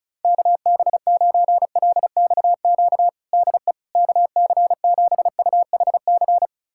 Morse code, heard as a keyed tone, about 36 wpm, 700 Hz. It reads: KB9LXQ DE KC7UHC
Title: USA Callsign DE Callsign - 6 Characters - 35wpm